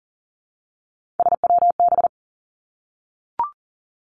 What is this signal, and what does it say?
Morse code, heard as a keyed tone, about 40 wpm, 700 Hz. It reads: SWB
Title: Three Challenging Letter-Number Combinations 40wpm